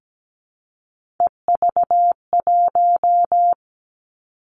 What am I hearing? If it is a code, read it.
Morse code, heard as a keyed tone, about 17 wpm, 700 Hz. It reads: EV1
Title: Three Challenging Letter-Number Combinations 17wpm